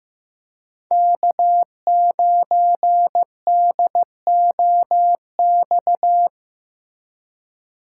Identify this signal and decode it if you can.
Morse code, heard as a keyed tone, about 15 wpm, 700 Hz. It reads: K9DOX
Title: USA Call Signs - 2x Character Spacing - 15wpm